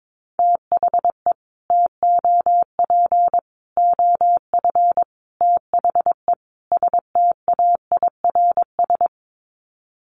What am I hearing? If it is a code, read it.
Morse code, heard as a keyed tone, about 22 wpm, 700 Hz. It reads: THE TOP OF THE STAIRS
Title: Sets of 5 Words Encore 22wpm